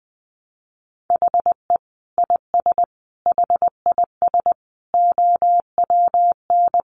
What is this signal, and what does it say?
Morse code, heard as a keyed tone, about 20 wpm, 700 Hz. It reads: HE IS HIS OWN
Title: Sentences from Top 1300 Words 20wpm